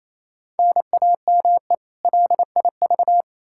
Morse code, heard as a keyed tone, about 28 wpm, 700 Hz. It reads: NAME LIV
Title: Practice QSO - Exchange 1 - Beginning - 28wpm